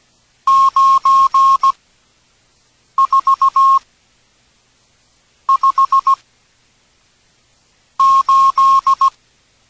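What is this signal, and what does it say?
Morse code, heard as a keyed tone, about 17 wpm, spaced as for 5 wpm, 1080 Hz. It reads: 9458
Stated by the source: Morse from an experimental text-to-speech project for radio amateurs.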